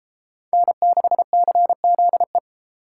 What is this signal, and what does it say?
Morse code, heard as a keyed tone, about 32 wpm, 700 Hz. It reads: N6CZE